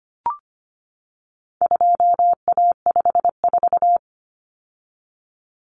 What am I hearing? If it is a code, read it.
Morse code, heard as a keyed tone, about 25 wpm, 700 Hz. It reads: 2A54
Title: Common Cut Numbers - 4-Digit 25wpm